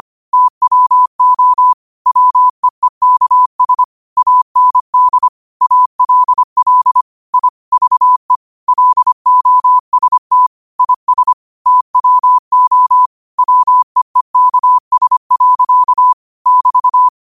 Morse code, heard as a keyed tone, about 25 wpm, 1000 Hz. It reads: TWO WEEKS AND ALL I VE LOST IS TWO WEEKS. <BT>
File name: qotdfiles / 25wpm